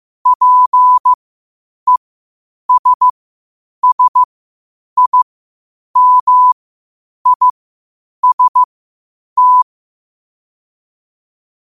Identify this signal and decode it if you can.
Morse code, heard as a keyed tone, about 15 wpm, spaced as for 8 wpm, 1000 Hz. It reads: PESSIMIST
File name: qotdfiles / 8wpm